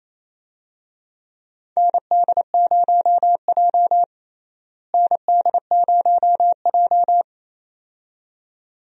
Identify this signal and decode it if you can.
Morse code, heard as a keyed tone, about 28 wpm, 700 Hz. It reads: ND0J ND0J
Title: USA Call Signs - 4 Characters - Repeated 2x - 28wpm